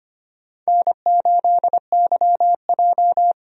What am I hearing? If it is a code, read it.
Morse code, heard as a keyed tone, about 25 wpm, 700 Hz. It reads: N8YJ
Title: USA Call Signs - 4 Characters - 25wpm